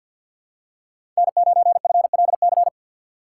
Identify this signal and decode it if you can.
Morse code, heard as a keyed tone, about 46 wpm, 700 Hz. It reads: N9FLC
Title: USA Call Signs - 5 Characters - 4x Character Spacing - 50wpm